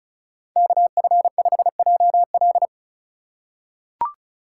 Morse code, heard as a keyed tone, about 36 wpm, 700 Hz. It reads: KF5JL
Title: USA Call Signs - 5 Characters - 3x Character Spacing - 35wpm